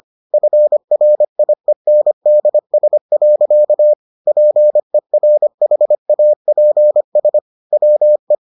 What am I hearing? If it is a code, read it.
Morse code, heard as a keyed tone, about 25 wpm, 600 Hz. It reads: FRIENDS. PERHAPS WE